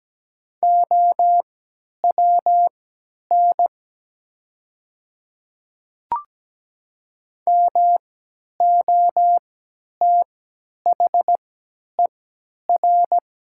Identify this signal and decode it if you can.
Morse code, heard as a keyed tone, about 17 wpm, spaced as for 10 wpm, 700 Hz. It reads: OWN MOTHER
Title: 3x Character Spacing - Top 300 Words - 17wpm